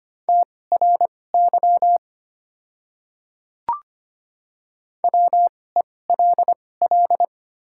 Morse code, heard as a keyed tone, about 25 wpm, spaced as for 18 wpm, 700 Hz. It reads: TRY WELL
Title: 2x Character Spacing - Top 200 Words - 25wpm